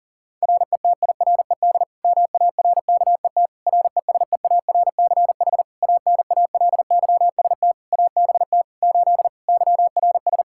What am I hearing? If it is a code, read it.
Morse code, heard as a keyed tone, about 40 wpm, 690 Hz. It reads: RETIRED MARKET RESEARCH ANALYST ABT 8 YRS
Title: Practice QSO - Exchange 3 - More Info - 40wpm